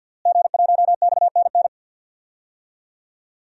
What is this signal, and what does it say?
Morse code, heard as a keyed tone, about 53 wpm, 700 Hz. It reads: G1XNN
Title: DXCC-100 Call Signs - 5 Characters - Repeated 2x - 50wpm